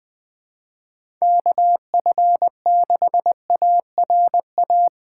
Morse code, heard as a keyed tone, about 20 wpm, 700 Hz. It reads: KF6ARA
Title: USA Call Signs - 6 Characters - 2x Character Spacing - 20wpm